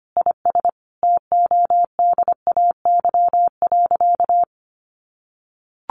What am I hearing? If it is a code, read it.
Morse code, heard as a keyed tone, about 25 wpm, 700 Hz. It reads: IS TODAY.
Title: Sentences from Top 1500 Words 25wpm